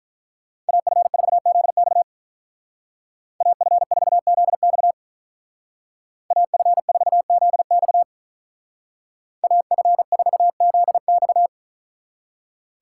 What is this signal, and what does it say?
Morse code, heard as a keyed tone, about 44 wpm, 700 Hz. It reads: AF4ZX AF4ZX AF4ZX AF4ZX
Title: Speed Racing - USA 2x2 Call Signs 35wpm